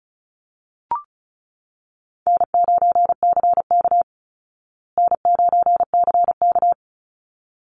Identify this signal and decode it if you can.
Morse code, heard as a keyed tone, about 34 wpm, 700 Hz. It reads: N9CK N9CK
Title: Call Signs Repeated 3x - 35wpm